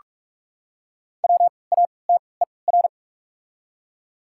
Morse code, heard as a keyed tone, about 46 wpm, spaced as for 26 wpm, 690 Hz. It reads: WATER
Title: 3x Character Spacing - Top 200 Words - 45wpm